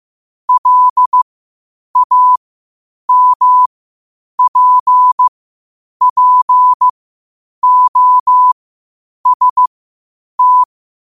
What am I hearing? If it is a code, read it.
Morse code, heard as a keyed tone, about 15 wpm, spaced as for 8 wpm, 1000 Hz. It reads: LAMPPOST
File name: qotdfiles / 8wpm